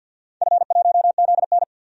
Morse code, heard as a keyed tone, about 46 wpm, 700 Hz. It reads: F1ZN
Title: DXCC-100 Call Signs - 4 Characters - 50wpm